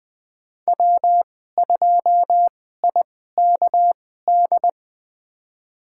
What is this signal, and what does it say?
Morse code, heard as a keyed tone, about 20 wpm, spaced as for 15 wpm, 700 Hz. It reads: W2IKD
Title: USA Call Signs - 5 Characters - 2x Character Spacing - 20wpm